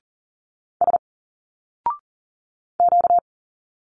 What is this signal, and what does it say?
Morse code, heard as a keyed tone, about 40 wpm, 700 Hz. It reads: S Q
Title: Single Letters 40wpm